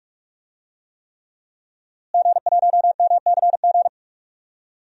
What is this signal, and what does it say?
Morse code, heard as a keyed tone, about 46 wpm, 700 Hz. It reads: G1MCG
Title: DXCC-100 Call Signs - 5 Characters - Repeated 3x - 45wpm